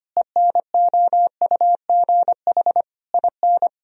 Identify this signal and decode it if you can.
Morse code, heard as a keyed tone, about 25 wpm, 700 Hz. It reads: ENOUGH IN
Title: Sentences from Top 1000 Words - Review Part 2 - 25wpm